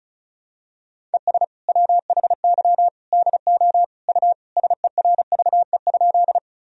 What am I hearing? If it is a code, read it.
Morse code, heard as a keyed tone, about 36 wpm, 700 Hz. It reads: ES WHY DO U SERVE?
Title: Sentences from Top 400 Words and Abbreviations 35wpm